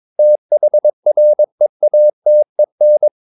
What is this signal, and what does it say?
Morse code, heard as a keyed tone, about 22 wpm, 600 Hz. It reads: THREATEN
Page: Romanen Never af Ken Follet på engelsk.